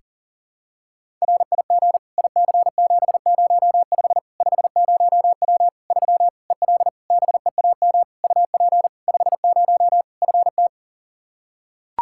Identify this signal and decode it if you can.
Morse code, heard as a keyed tone, about 40 wpm, 700 Hz. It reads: RIG IC705 50W 3 EL BEAM UP 50 FT
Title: Practice QSO - Exchange 2 - Info - 40wpm